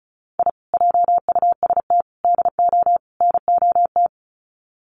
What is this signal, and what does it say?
Morse code, heard as a keyed tone, about 34 wpm, 700 Hz. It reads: I JUST DO NOT
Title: Sets of 4 Words 35wpm